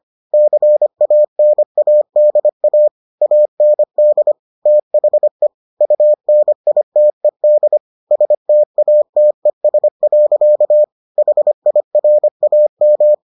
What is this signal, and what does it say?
Morse code, heard as a keyed tone, about 25 wpm, 600 Hz. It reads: CANADA AND THE UNITED STATES. HIRAM